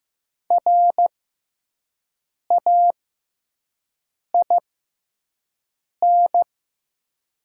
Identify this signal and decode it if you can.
Morse code, heard as a keyed tone, about 15 wpm, spaced as for 5 wpm, 700 Hz. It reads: RAIN
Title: 6x Character Spacing - Top 300 Words - 15wpm